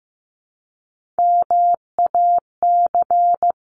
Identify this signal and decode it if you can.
Morse code, heard as a keyed tone, about 15 wpm, 700 Hz. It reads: MAC
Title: CWT Exchange 15wpm